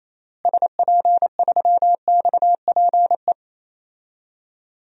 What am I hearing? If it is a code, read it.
Morse code, heard as a keyed tone, about 28 wpm, 700 Hz. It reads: SP3XPE
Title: DXCC-100 Call Signs - 6 Characters - Repeated 2x - 28wpm